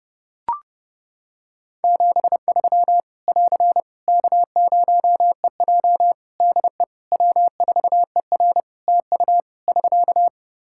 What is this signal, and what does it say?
Morse code, heard as a keyed tone, about 32 wpm, 700 Hz. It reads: 73 <AR> K0EJ DE W4ER TU <SK>
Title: Practice QSO - Exchange 4 - Ending - 30wpm